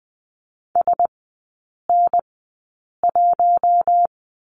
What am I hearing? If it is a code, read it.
Morse code, heard as a keyed tone, about 20 wpm, spaced as for 8 wpm, 700 Hz. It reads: SN1
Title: 041 Lesson 8e - Number 1 - Three Characters 20wpm